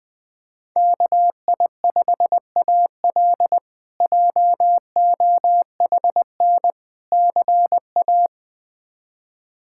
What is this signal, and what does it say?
Morse code, heard as a keyed tone, about 20 wpm, 700 Hz. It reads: KI5AL JOHN CA
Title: K1USN SST QSO Practice - 20wpm